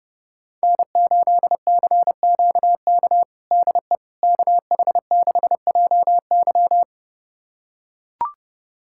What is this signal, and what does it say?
Morse code, heard as a keyed tone, about 30 wpm, 700 Hz. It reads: N8CQK DE KH6JY
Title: USA Callsign DE Callsign - 5 Characters - 30wpm